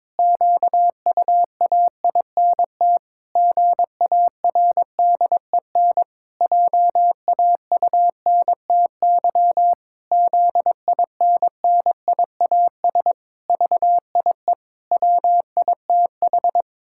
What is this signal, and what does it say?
Morse code, heard as a keyed tone, about 21 wpm, 700 Hz. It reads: QUAINT GARDEN JAUNTY ZINNIAS VIE WITH
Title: Pangram Challenge v2 - 22wpm